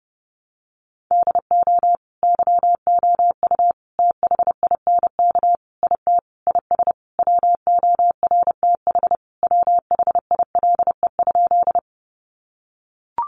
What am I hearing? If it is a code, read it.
Morse code, heard as a keyed tone, about 28 wpm, 700 Hz. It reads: DO YOU THINK IT IS WORTH WHILE?
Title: Sentences from Top 800 Words 30wpm